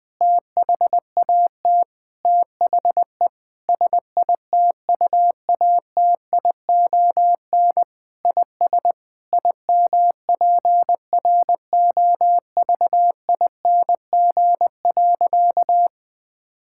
Morse code, heard as a keyed tone, about 20 wpm, 700 Hz. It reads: THAT THE SITUATION IS IMPROVING.
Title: Sentences with 8-Word Phrases - 20wpm